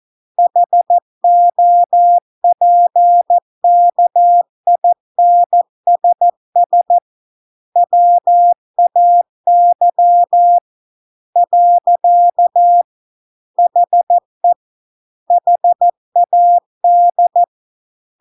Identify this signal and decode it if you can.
Morse code, heard as a keyed tone, about 14 wpm, 700 Hz. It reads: HOPKINSS WAY . HE HAD